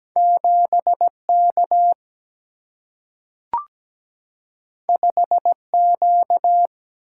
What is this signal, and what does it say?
Morse code, heard as a keyed tone, about 17 wpm, 700 Hz. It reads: 7K 5Q
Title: Single-Letter Single-Number 17wpm